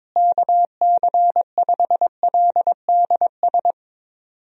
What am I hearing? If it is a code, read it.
Morse code, heard as a keyed tone, about 22 wpm, 700 Hz. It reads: KC5LDS
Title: USA Call Signs - 6 Characters - Repeated 3x - 22wpm